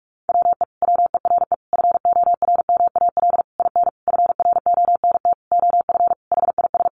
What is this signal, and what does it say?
Morse code, heard as a keyed tone, about 44 wpm, 700 Hz. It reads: WE WERE FORMAL IN FRONT OF HIS
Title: Sentences from Top 1300 Words 45wpm